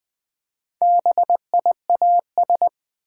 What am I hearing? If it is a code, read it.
Morse code, heard as a keyed tone, about 20 wpm, 700 Hz. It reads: BIAS
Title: Build A Word - 1 Letter At A Time - 4-Letter Words - 20wpm